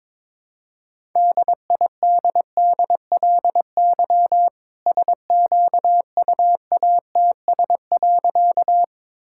Morse code, heard as a keyed tone, about 22 wpm, 700 Hz. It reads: DIDDLY SQUATS.
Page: morse-code-ninja-practice-22wpm